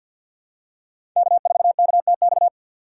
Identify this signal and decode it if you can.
Morse code, heard as a keyed tone, about 53 wpm, 690 Hz. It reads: K4KTX
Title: USA Call Signs - 2x Character Spacing - 50wpm